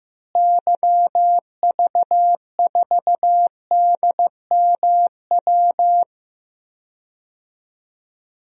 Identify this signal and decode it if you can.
Morse code, heard as a keyed tone, about 15 wpm, 700 Hz. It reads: YV4DMW
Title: DXCC-100 Call Signs - 6 Characters - Repeated 2x - 15wpm